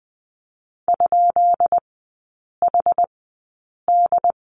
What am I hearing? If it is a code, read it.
Morse code, heard as a keyed tone, about 20 wpm, spaced as for 8 wpm, 700 Hz. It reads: ?HD
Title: 191 Lesson 31e - Letter v - Three Characters 20wpm